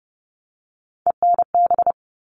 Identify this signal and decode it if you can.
Morse code, heard as a keyed tone, about 30 wpm, 700 Hz. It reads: ENB
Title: All Cut Numbers - 3-Digit 30wpm